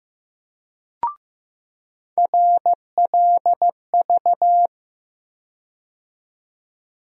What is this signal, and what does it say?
Morse code, heard as a keyed tone, about 15 wpm, 700 Hz. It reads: RLV
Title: USA Call Sign Postfixes - 3 Letters 15wpm